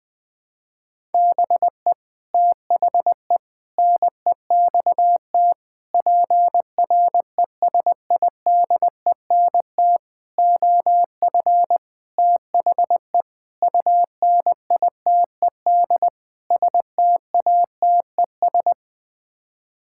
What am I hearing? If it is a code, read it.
Morse code, heard as a keyed tone, about 20 wpm, 700 Hz. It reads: BE THE NEXT PRESIDENT OF THE UNITED STATES
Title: Sets of 8 Words - 20wpm